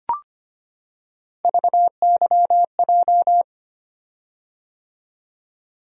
Morse code, heard as a keyed tone, about 25 wpm, 700 Hz. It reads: VYJ
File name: Three Letter-Number Combinations - 25wpm